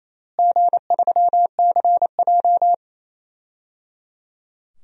Morse code, heard as a keyed tone, about 28 wpm, 700 Hz. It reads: G3CJ
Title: DXCC-100 Call Signs - 4 Characters - Repeated 3x - 28wpm